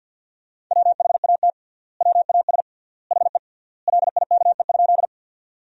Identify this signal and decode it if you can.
Morse code, heard as a keyed tone, about 50 wpm, 700 Hz. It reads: WHAT WAS HE LIKE?
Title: 3x Word-Spacing - Sentences from Top 100 Words - 50wpm